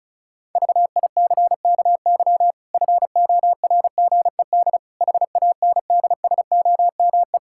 Morse code, heard as a keyed tone, about 34 wpm, 700 Hz. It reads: VICKY FORGED HANDSOME
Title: Pangram Challenge v2 - 35wpm